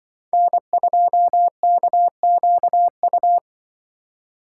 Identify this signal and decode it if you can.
Morse code, heard as a keyed tone, about 24 wpm, 700 Hz. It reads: N2KQU
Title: Speed Racing - USA 1x3 Call Signs 20wpm